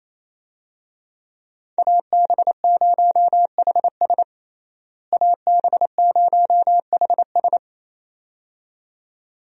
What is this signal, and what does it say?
Morse code, heard as a keyed tone, about 28 wpm, 700 Hz. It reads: AB0HS AB0HS
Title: USA Call Signs - 5 Characters - Repeated 2x - 28wpm